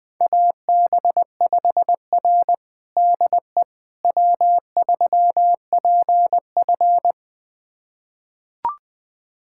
Morse code, heard as a keyed tone, about 21 wpm, 700 Hz. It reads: AB5R DE W3PF
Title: USA Callsign DE Callsign - 4 Characters - 20wpm